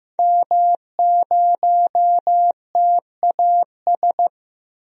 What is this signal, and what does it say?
Morse code, heard as a keyed tone, about 15 wpm, 700 Hz. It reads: M0TAS
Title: DXCC-100 Call Signs - 5 Characters - 15wpm